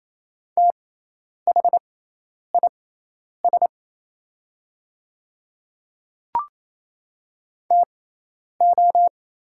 Morse code, heard as a keyed tone, about 27 wpm, spaced as for 10 wpm, 690 Hz. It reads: THIS TO
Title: 6x Character Spacing - Top 100 Words - 28wpm